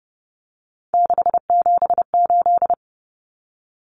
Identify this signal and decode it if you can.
Morse code, heard as a keyed tone, about 30 wpm, 700 Hz. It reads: 678
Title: Common Cut Numbers - 3-Digit 30wpm